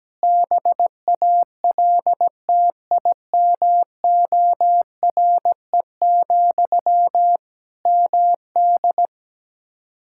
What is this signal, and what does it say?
Morse code, heard as a keyed tone, about 17 wpm, 700 Hz. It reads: BALTIMORE, MD
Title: Practice QSO - Exchange 1 - Beginning - 17wpm